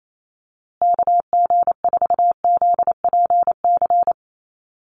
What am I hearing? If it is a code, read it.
Morse code, heard as a keyed tone, about 28 wpm, 700 Hz. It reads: KG4ZPC
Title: USA Call Signs Repeated 2x 28wpm